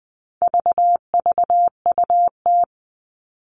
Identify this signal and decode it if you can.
Morse code, heard as a keyed tone, about 20 wpm, 700 Hz. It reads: VVUT
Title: All Cut Numbers - 4-Digit 20wpm